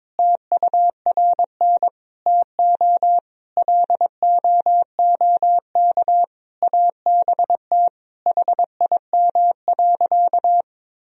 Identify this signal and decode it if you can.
Morse code, heard as a keyed tone, about 22 wpm, 700 Hz. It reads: TURN TO LOOK ABT HIM.
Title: Sentences from Top 200 Words and Abbreviations 22wpm